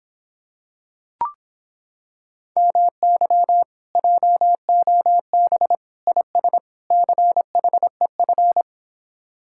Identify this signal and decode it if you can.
Morse code, heard as a keyed tone, about 26 wpm, 700 Hz. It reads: MY JOB IS CHEF